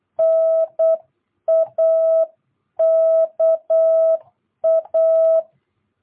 Morse code, heard as a keyed tone, about 8 wpm, 650 Hz. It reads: NAKA
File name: jh8cbh / 2022cw